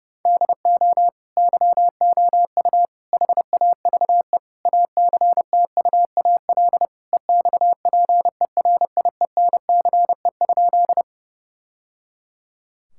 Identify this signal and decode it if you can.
Morse code, heard as a keyed tone, about 30 wpm, 700 Hz. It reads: DO YOU HAVE ACTUAL EXPERIENCE?
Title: Sentences from Top 1000 Words - Review Part 5 - 30wpm